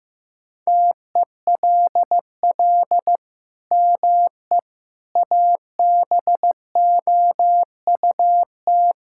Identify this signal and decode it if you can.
Morse code, heard as a keyed tone, about 14 wpm, 700 Hz. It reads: TELL ME ABOUT
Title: Sentences from Top 1000 Words - Review Part 5 - 15wpm